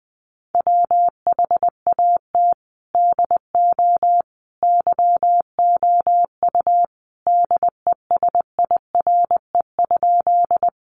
Morse code, heard as a keyed tone, about 20 wpm, 700 Hz. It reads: WHAT DO YOU DESIRE?
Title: Sentences from Top 1100 Words 20wpm